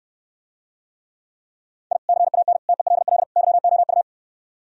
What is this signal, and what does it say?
Morse code, heard as a keyed tone, about 67 wpm, 700 Hz. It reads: I CAN TELL YOU